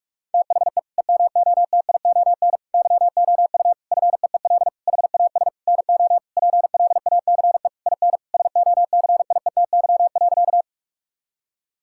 Morse code, heard as a keyed tone, about 46 wpm, 700 Hz. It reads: THE EMOTION YOU FEEL HAS NO PLACE IN SOCIETY.
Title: Sentences from Top 1600 Words 45wpm